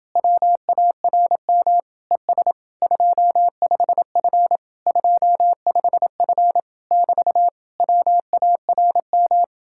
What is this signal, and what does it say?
Morse code, heard as a keyed tone, about 30 wpm, 700 Hz. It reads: WARM ES 25F 25F <BT> WARM